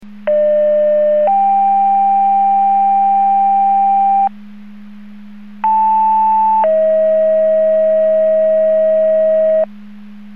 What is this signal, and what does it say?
Two-tone page, heard as a one-second tone, then a three-second tone, about 600 then 790 Hz, about 880 then 640 Hz. It is heard from 0.3 to 9.6 s.
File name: Portage / 00429 23-40 Portage 154.385 Paging